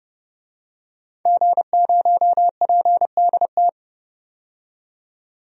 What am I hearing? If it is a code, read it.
Morse code, heard as a keyed tone, about 32 wpm, 700 Hz. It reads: G0PDT
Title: DXCC-100 Call Signs - 5 Characters - 30wpm